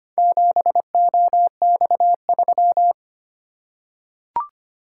Morse code, heard as a keyed tone, about 25 wpm, 700 Hz. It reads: 7OX3
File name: practice / 25wpm / Four Letter-Number Combinations - 25wpm